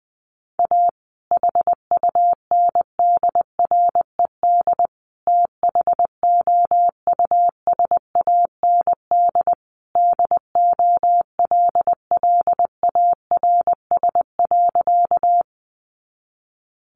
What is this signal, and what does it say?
Morse code, heard as a keyed tone, about 20 wpm, 700 Hz. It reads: A HUNDRED THOUSAND DOLLARS.
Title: A Sound Of Thunder 20wpm